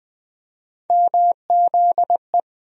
Morse code, heard as a keyed tone, about 20 wpm, 700 Hz. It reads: MZE
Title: USA Call Sign Postfixes - 3 Letters 20wpm